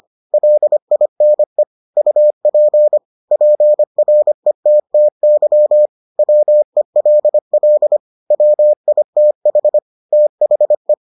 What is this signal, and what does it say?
Morse code, heard as a keyed tone, about 25 wpm, 600 Hz. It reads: LINE UP PRETTY WELL WITH THE